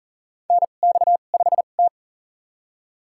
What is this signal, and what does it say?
Morse code, heard as a keyed tone, about 37 wpm, spaced as for 28 wpm, 700 Hz. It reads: NX5T
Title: USA Call Signs - 4 Characters - 2x Character Spacing - 40wpm